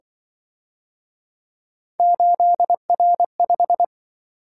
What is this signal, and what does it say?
Morse code, heard as a keyed tone, about 25 wpm, 700 Hz. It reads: 8R5
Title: Speed-Racing - Three Letter-Number Combinations - 20wpm